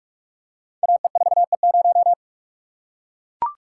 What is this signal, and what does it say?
Morse code, heard as a keyed tone, about 44 wpm, 700 Hz. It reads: AE4E0
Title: Code Groups - 5 Letter:Numbers - 45wpm